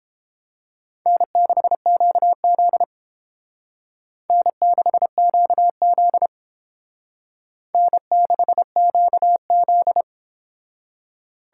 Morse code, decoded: N6QZ N6QZ N6QZ

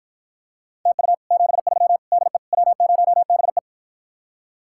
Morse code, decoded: TU 73 DE W0BE

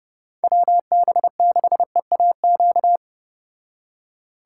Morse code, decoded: WB6EAQ

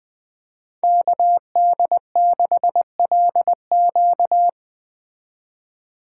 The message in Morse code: KD6LQ

